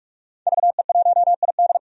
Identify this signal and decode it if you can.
Morse code, heard as a keyed tone, about 46 wpm, 700 Hz. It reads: VE1ID
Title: DXCC-100 Call Signs - 5 Characters - 45wpm